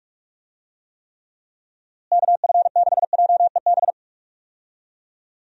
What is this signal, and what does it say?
Morse code, heard as a keyed tone, about 42 wpm, 700 Hz. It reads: KF6JEB
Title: USA Call Signs - 6 Characters - Repeated 2x - 45wpm